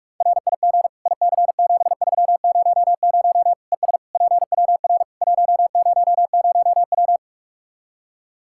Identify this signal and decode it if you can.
Morse code, heard as a keyed tone, about 43 wpm, 700 Hz. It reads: RIG IC7300 ES PWR 100W
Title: Practice QSO - Exchange 2 - Info - 45wpm